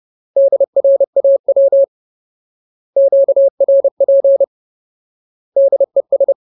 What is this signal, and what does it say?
Morse code, heard as a keyed tone, about 30 wpm, 550 Hz. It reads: DRAW QRP DES
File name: Ultimate Word List / 30WPM Ultimate